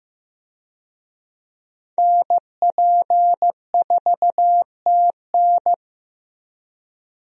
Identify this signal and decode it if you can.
Morse code, heard as a keyed tone, about 15 wpm, 700 Hz. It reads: NP4TN
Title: USA Call Signs - 5 Characters - Repeated 2x - 15wpm